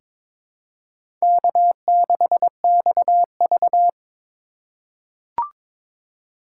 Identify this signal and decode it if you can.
Morse code, heard as a keyed tone, about 22 wpm, 700 Hz. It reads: K6XV
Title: USA Call Signs - 4 Characters - 4x Character Spacing - 22wpm